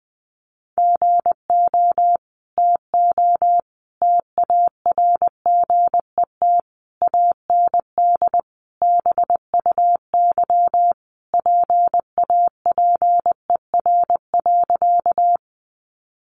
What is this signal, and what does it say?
Morse code, heard as a keyed tone, about 20 wpm, 700 Hz. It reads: GO TO TARGET AND BUY PAPER.
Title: Sentences from Top 1200 Words 20wpm